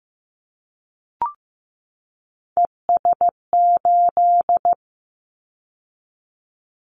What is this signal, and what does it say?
Morse code, heard as a keyed tone, about 15 wpm, 700 Hz. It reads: ES8
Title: Callsign Prefixes for Top 50 DXCC - 15wpm